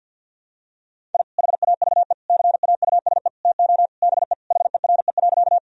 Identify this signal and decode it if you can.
Morse code, heard as a keyed tone, about 51 wpm, 700 Hz. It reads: I HAVE CAUSE TO BE HERE.
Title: Sentences from Top 1000 Words - Review Part 1 - 50wpm